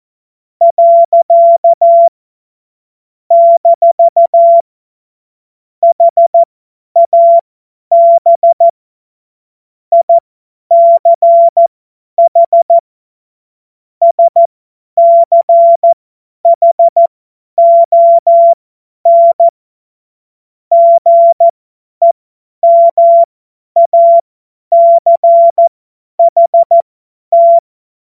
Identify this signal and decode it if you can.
Morse code, heard as a keyed tone, about 14 wpm, spaced as for 10 wpm, 680 Hz. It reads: . - HAB ICH SCHON GEMACHT
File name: Morsekurs / Witze / Witz 29 April 2025 50 F70